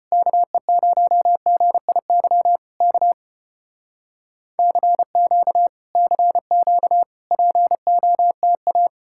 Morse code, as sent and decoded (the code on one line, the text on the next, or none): -.- . ----- --. .. -.-- / -.- / -.-. --.- / -.-. --.- / .--. --- - .-
KE0GIY K CQ CQ POTA